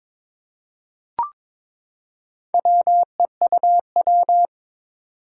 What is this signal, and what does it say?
Morse code, heard as a keyed tone, about 22 wpm, 700 Hz. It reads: WEUW
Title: Four Challenging Letter-Number Combinations 22wpm